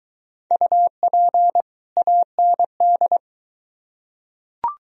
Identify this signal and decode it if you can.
Morse code, heard as a keyed tone, about 22 wpm, 700 Hz. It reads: UP AND